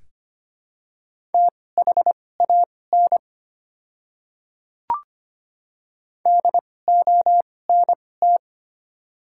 Morse code, heard as a keyed tone, about 25 wpm, spaced as for 18 wpm, 700 Hz. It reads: THAN DONT
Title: 2x Character Spacing - Top 300 Words - 25wpm